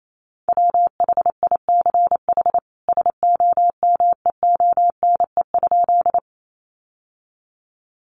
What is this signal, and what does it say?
Morse code, heard as a keyed tone, about 28 wpm, 700 Hz. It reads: WHICH SOMEONE?
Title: Sentences from Top 600 Words 28wpm